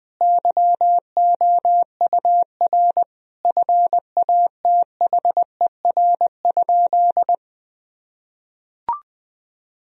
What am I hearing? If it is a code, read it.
Morse code, heard as a keyed tone, about 20 wpm, 700 Hz. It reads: YOUR FATHER?
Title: Sentences from Top 1000 Words - Review Part 3 - 20wpm